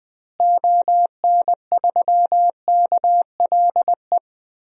Morse code, heard as a keyed tone, about 20 wpm, 700 Hz. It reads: ON3KLE
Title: DXCC-100 Call Signs - 6 Characters - Repeated 2x - 20wpm